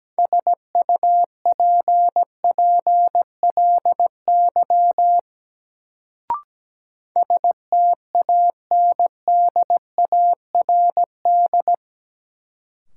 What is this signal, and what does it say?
Morse code, heard as a keyed tone, about 17 wpm, 700 Hz. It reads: SUPPLY STANDARD
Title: Sentences from Top 1200 Words - New Word Drill Alt - 17wpm